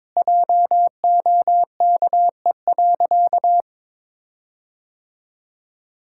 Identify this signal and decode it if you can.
Morse code, heard as a keyed tone, about 22 wpm, 690 Hz. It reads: JOKE.
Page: morse-code-ninja-practice-22wpm